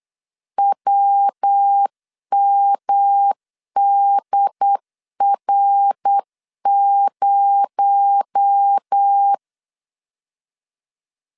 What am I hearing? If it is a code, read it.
Morse code, heard as a keyed tone, about 8 wpm, 800 Hz. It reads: WMDR0
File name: LEZIONI / 7 GF9Z / 7 G 8wpm 3118